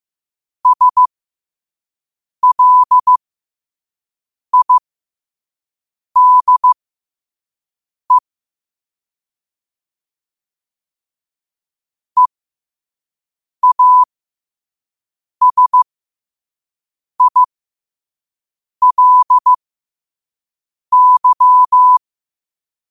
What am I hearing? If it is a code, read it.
Morse code, heard as a keyed tone, about 15 wpm, spaced as for 5 wpm, 1000 Hz. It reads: SLIDE EASILY
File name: qotdfiles / 5wpm